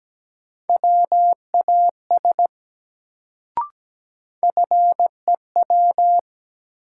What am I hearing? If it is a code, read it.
Morse code, heard as a keyed tone, about 17 wpm, 700 Hz. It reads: WAS FEW
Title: Sets of 2 Words - New Word Drill - 17wpm